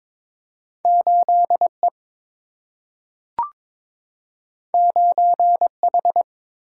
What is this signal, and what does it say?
Morse code, heard as a keyed tone, about 22 wpm, 700 Hz. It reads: 8 E 9 H
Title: Single-Letter Single-Number 22wpm